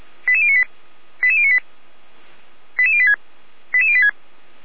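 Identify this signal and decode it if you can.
Five-tone selective call (ZVEI), heard as quick runs of five tones, tones of about 70 ms. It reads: 88098 88098 88086 88086
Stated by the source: ZVEI five-tone selective call received on SDR, from the Feuerwehr Orlamünde audio server.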